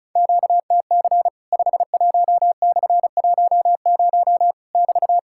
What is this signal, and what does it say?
Morse code, heard as a keyed tone, about 36 wpm, 700 Hz. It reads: QTC 51/10 <BT>
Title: Worked All Europe DX Contest - QTC Exchange - 35wpm